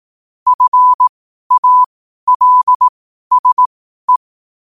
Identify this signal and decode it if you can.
Morse code, heard as a keyed tone, about 18 wpm, spaced as for 12 wpm, 1000 Hz. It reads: FALSE
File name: qotdfiles / 12wpm